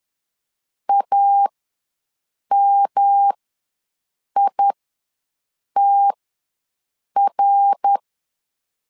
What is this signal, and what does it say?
Morse code, heard as a keyed tone, about 11 wpm, spaced as for 6 wpm, 800 Hz. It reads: AMITR
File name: LEZIONI / 3 MWRO / 3 MWR ws 10wpm 2151